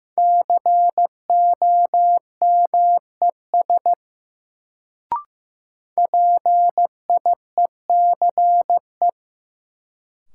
Morse code, heard as a keyed tone, about 16 wpm, 700 Hz. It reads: COMES PIECE
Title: Sets of 3 Words - New Word Drill - 15wpm